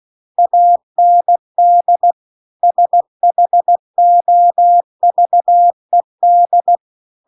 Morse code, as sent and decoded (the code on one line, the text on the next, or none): .- -. -.. / ... .... --- ...- . -..
AND SHOVED